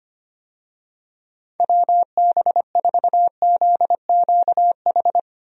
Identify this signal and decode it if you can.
Morse code, heard as a keyed tone, about 25 wpm, 700 Hz. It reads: WB4ZQH
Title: USA Call Signs - 6 Characters - 25wpm